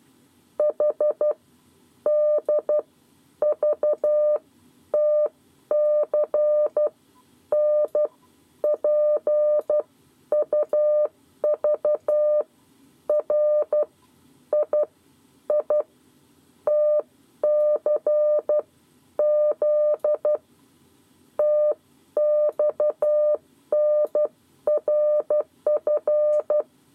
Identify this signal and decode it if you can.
Morse code, heard as a keyed tone, about 12 wpm, spaced as for 9 wpm, 600 Hz. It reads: HDVTC NPUVR IITCZ TXNRF
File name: 7 Lezione cw 2